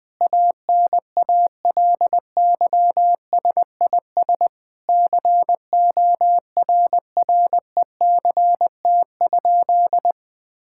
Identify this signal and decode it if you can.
Morse code, heard as a keyed tone, about 20 wpm, 700 Hz. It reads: ANALYSIS CORRECT?
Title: Sentences from Top 1000 Words - Review Part 5 - 20wpm